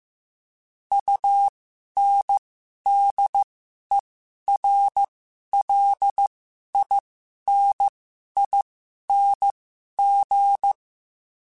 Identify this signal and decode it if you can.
Morse code, heard as a keyed tone, about 15 wpm, spaced as for 11 wpm, 800 Hz. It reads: UNDERLINING
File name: Special INGs-mp3s / Special INGs-mp3s / Sngl Ing @10